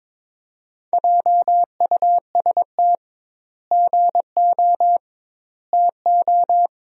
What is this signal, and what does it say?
Morse code, heard as a keyed tone, about 22 wpm, 700 Hz. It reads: JUST GO TO
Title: 2x Word-Spacing - Sentences from Top 500 Words - 22wpm